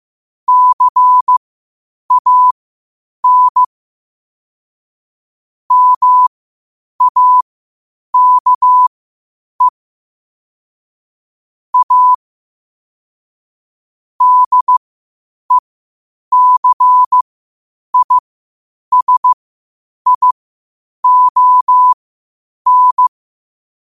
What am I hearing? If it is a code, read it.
Morse code, heard as a keyed tone, about 15 wpm, spaced as for 8 wpm, 1000 Hz. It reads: CAN MAKE A DECISION
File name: qotdfiles / 8wpm